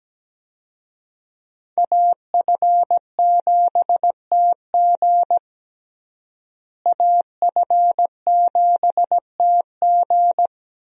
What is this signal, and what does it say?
Morse code, heard as a keyed tone, about 17 wpm, 700 Hz. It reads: AF7TG AF7TG
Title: USA Call Signs - 5 Characters - Repeated 3x - 17wpm